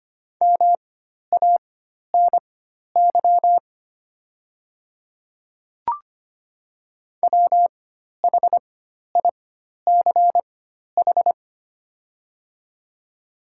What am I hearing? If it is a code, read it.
Morse code, heard as a keyed tone, about 25 wpm, spaced as for 12 wpm, 700 Hz. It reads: MANY WHICH
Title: 4x Character Spacing - Top 100 Words - 25wpm